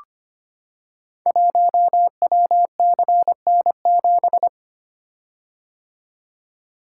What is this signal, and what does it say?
Morse code, heard as a keyed tone, about 25 wpm, 700 Hz. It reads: 1WCN7
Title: Code Groups - 5 Letter:Numbers - 25wpm